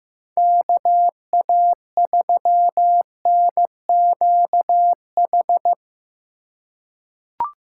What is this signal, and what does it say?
Morse code, heard as a keyed tone, about 15 wpm, 700 Hz. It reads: KA3NQH
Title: USA Call Signs - 6 Characters - 2x Character Spacing - 15wpm